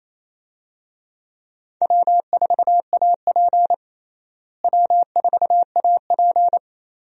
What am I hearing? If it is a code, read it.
Morse code, heard as a keyed tone, about 27 wpm, 700 Hz. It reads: W4AP W4AP
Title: USA Call Signs - 4 Characters - Repeated 3x - 28wpm